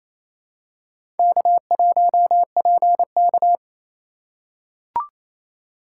Morse code, heard as a keyed tone, about 27 wpm, 700 Hz. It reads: K1PK
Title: USA Call Signs - 3x Character Spacing - 28wpm